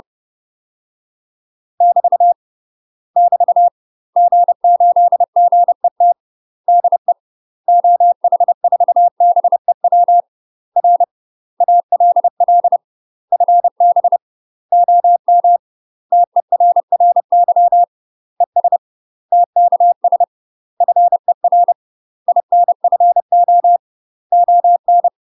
Morse code, heard as a keyed tone, about 30 wpm, 700 Hz. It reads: <BT> <BT> G8GET DE OH4BEW R ALL FB OM TERRY ES TKS FER INFO ON
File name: UK MASTER QSOs TRAINING / 30WPM / UK MASTER QSOs 0042